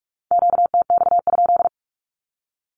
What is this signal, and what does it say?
Morse code, heard as a keyed tone, about 46 wpm, 700 Hz. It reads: QTX?
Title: Q-Codes with Spoken Meaning - 45wpm